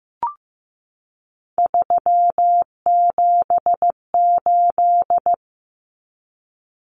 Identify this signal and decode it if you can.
Morse code, heard as a keyed tone, about 15 wpm, 700 Hz. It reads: 378
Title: Common Cut Numbers - 3-Digit 15wpm